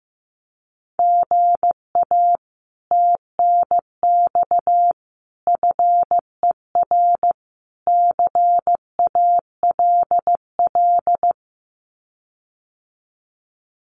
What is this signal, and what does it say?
Morse code, heard as a keyed tone, about 15 wpm, 700 Hz. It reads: GA TNX FER CALL
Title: Common QSO Phrases Encore 15wpm